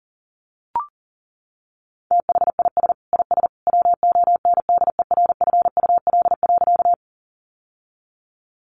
Morse code, heard as a keyed tone, about 37 wpm, 700 Hz. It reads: THIS IS WONDERFUL.
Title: Sentences from Top 1100 Words 40wpm